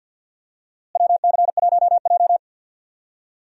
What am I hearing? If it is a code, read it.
Morse code, heard as a keyed tone, about 50 wpm, 700 Hz. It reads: WC1J
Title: USA Call Signs - 4 Characters - Repeated 3x - 50wpm